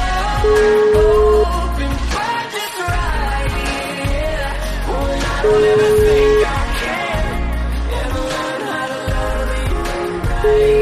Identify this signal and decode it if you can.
Ringback tone (Europe, 425 Hz), with noise.